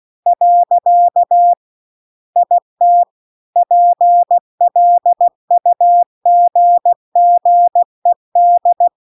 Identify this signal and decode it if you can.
Morse code, heard as a keyed tone, about 16 wpm, 700 Hz. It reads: . IT PLUGGED
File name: CW-The-Big-Twoheaded-River-2-16WPM